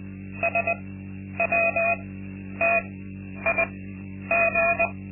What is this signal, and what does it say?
Morse code, heard as a keyed tone, about 20 wpm, spaced as for 10 wpm, 680 Hz. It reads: SWTIG
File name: Linksmakalnis-2